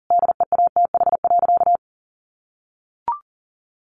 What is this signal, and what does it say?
Morse code, heard as a keyed tone, about 40 wpm, 700 Hz. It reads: DEATH.
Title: Sentences from Top 1400 Words 40wpm